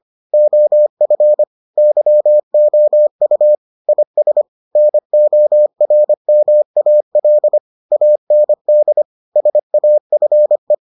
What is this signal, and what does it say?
Morse code, heard as a keyed tone, about 25 wpm, 600 Hz. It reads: OF YOU IS NORMAL AND SAFE